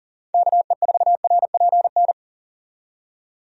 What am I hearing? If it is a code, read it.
Morse code, heard as a keyed tone, about 40 wpm, 700 Hz. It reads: KE4RPN
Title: USA Call Signs - 6 Characters - 40wpm